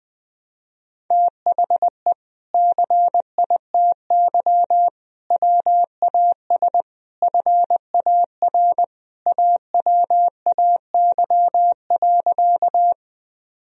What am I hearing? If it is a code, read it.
Morse code, heard as a keyed tone, about 19 wpm, 700 Hz. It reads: THE CITY WAS FAR AWAY.
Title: Sentences from Top 1000 Words - Review Part 1 - 20wpm